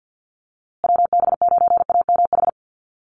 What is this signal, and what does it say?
Morse code, heard as a keyed tone, about 50 wpm, 700 Hz. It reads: WB9AMH
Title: USA Call Signs Repeated 3x 50wpm